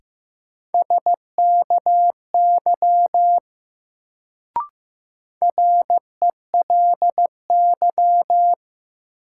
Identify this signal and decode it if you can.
Morse code, heard as a keyed tone, about 15 wpm, 700 Hz. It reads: SKY RELY
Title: Sentences from Top 1200 Words - New Word Drill - 15wpm